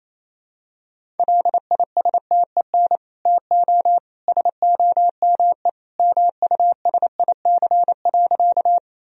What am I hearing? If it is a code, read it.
Morse code, heard as a keyed tone, about 28 wpm, 700 Hz. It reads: LISTEN TO SOME MUSIC.
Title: Sentences from Top 1000 Words - Review Part 1 - 28wpm